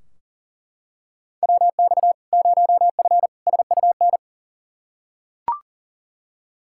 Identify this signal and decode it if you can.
Morse code, heard as a keyed tone, about 40 wpm, 700 Hz. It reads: WX 0F SUN
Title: Practice QSO - Exchange 2 - Info - 40wpm